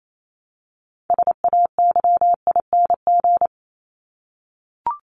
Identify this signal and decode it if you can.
Morse code, heard as a keyed tone, about 28 wpm, 700 Hz. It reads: SAYING
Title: ING Suffix - Top 100 Words with Base Words 28wpm